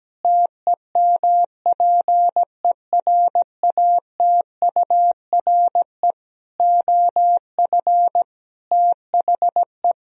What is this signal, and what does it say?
Morse code, heard as a keyed tone, about 17 wpm, 700 Hz. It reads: TEMPERATURE OF THE
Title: Sentences from Top 1000 Words - Review Part 4 - 17wpm